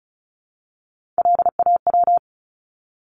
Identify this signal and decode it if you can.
Morse code, heard as a keyed tone, about 34 wpm, 700 Hz. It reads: LAW
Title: ES:S Suffix - Top 100 Words with Base Words 35wpm